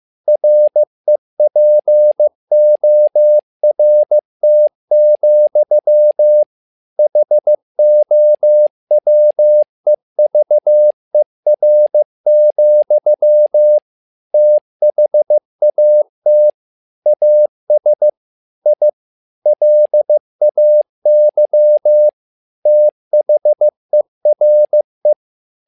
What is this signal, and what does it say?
Morse code, heard as a keyed tone, about 15 wpm, 600 Hz. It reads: REPORT, HOWEVER, THAT AS I LAY THERE
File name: learning center / tower trouble / tower trouble 15wpm